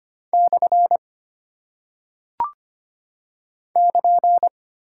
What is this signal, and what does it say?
Morse code, heard as a keyed tone, about 25 wpm, 700 Hz. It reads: / <KN>
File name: Single Punctuation-Prosign - 25wpm